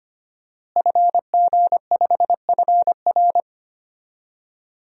Morse code, heard as a keyed tone, about 25 wpm, 700 Hz. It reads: FG5FR
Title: DXCC-100 Call Signs - 5 Characters - Repeated 2x - 25wpm